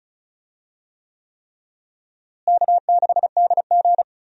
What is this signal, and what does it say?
Morse code, heard as a keyed tone, about 34 wpm, 700 Hz. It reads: K6DG
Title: USA Call Signs - 4 Characters - Repeated 3x - 35wpm